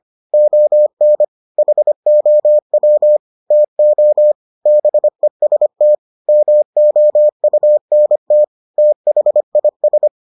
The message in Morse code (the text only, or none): ON HOW TO BEST MOUNT THIS